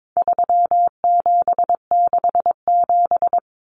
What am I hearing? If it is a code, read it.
Morse code, heard as a keyed tone, about 22 wpm, 700 Hz. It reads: 3767
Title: Numbers - 4-Digit 22wpm